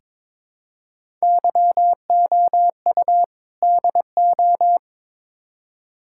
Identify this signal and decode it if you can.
Morse code, heard as a keyed tone, about 22 wpm, 700 Hz. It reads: YOU DO
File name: practice / 22wpm / Build A Phrase - 5-Word Phrases - 22wpm